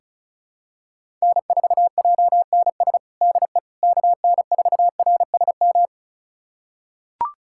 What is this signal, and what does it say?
Morse code, heard as a keyed tone, about 36 wpm, 700 Hz. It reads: N4JNS DE KN4RSM
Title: USA Callsign DE Callsign 35wpm